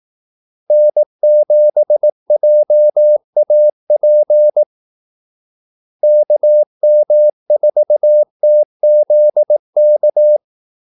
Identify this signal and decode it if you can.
Morse code, heard as a keyed tone, about 18 wpm, 600 Hz. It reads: N7JAP KM4TZK
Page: Kaldesignaler 18 wpm